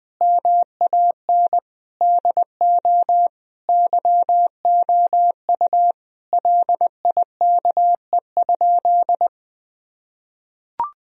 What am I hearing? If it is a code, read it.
Morse code, heard as a keyed tone, about 21 wpm, 700 Hz. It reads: MAN DO YOU LIKE?
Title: Sentences from Top 1000 Words - Review Part 3 - 20wpm